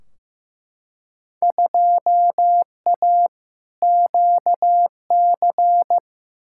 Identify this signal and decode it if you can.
Morse code, heard as a keyed tone, about 15 wpm, 700 Hz. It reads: 2A QC
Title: Field Day Exchanges - 15wpm